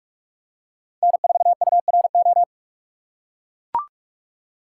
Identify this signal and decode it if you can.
Morse code, heard as a keyed tone, about 44 wpm, 690 Hz. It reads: N4URO
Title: USA Call Signs - 5 Characters - 2x Character Spacing - 45wpm